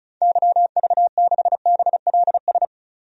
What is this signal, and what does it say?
Morse code, heard as a keyed tone, about 36 wpm, 700 Hz. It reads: YV6BLS
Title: DXCC-100 Call Signs - 6 Characters - 35wpm